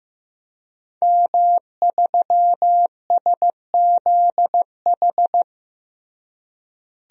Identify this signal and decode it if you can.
Morse code, heard as a keyed tone, about 15 wpm, 700 Hz. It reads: M3SZH
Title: DXCC-100 Call Signs - 5 Characters - Repeated 3x - 15wpm